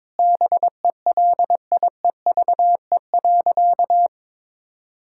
Morse code, heard as a keyed tone, about 22 wpm, 700 Hz. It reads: BELIEVE.
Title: Sentences from Top 1000 Words - Review Part 6 - 22wpm